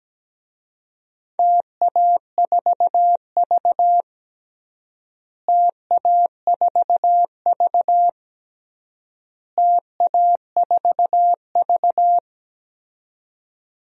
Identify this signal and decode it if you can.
Morse code, heard as a keyed tone, about 17 wpm, 700 Hz. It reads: TA4V TA4V TA4V
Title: DXCC-100 Call Signs - 4 Characters - Repeated 3x - 17wpm